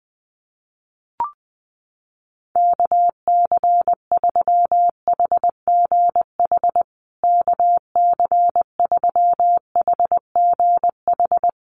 Morse code, heard as a keyed tone, about 20 wpm, 700 Hz. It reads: KC3HGH KC3HGH
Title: USA Call Signs Repeated 2x 20wpm